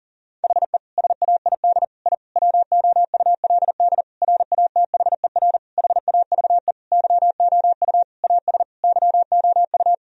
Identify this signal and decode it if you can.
Morse code, heard as a keyed tone, about 37 wpm, 700 Hz. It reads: HE SAID I WOULD RATHER HAVE YOU AS YOU
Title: Sentences from Top 1000 Words - Review Part 4 - 40wpm